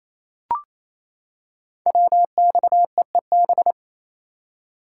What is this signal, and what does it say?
Morse code, heard as a keyed tone, about 28 wpm, 700 Hz. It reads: WXEEB
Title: Code Groups - 5 Letters - 28wpm